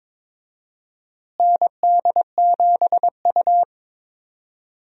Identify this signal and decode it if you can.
Morse code, heard as a keyed tone, about 22 wpm, 700 Hz. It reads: ND7U
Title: USA Call Signs - 4 Characters - Repeated 3x - 22wpm